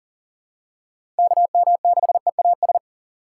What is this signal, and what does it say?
Morse code, heard as a keyed tone, about 38 wpm, 700 Hz. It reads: KM6EAS
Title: USA Call Signs - 6x Character Spacing - 40wpm